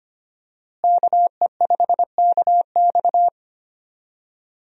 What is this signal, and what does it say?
Morse code, heard as a keyed tone, about 25 wpm, 700 Hz. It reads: KE5KX
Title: USA Call Signs - 5 Characters - Repeated 3x - 25wpm